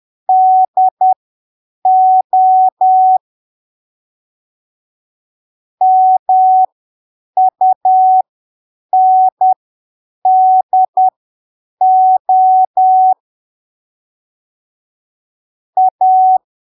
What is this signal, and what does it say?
Morse code, heard as a keyed tone, about 10 wpm, spaced as for 7 wpm, 750 Hz. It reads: DO MUNDO A
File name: arquivos / texto / texto 16 10ppm